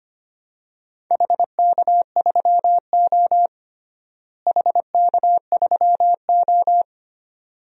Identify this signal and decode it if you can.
Morse code, heard as a keyed tone, about 25 wpm, 700 Hz. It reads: HK3O HK3O
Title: DXCC-100 Call Signs - 4 Characters - Repeated 2x - 25wpm